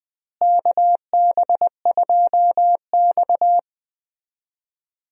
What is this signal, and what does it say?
Morse code, heard as a keyed tone, about 20 wpm, 700 Hz. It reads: KB2X
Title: USA Call Signs - 4 Characters - Repeated 3x - 20wpm